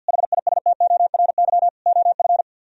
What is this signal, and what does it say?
Morse code, heard as a keyed tone, about 60 wpm, 700 Hz. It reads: HISTORY OF
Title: Speed-Racing - Sets of 4 Words 50wpm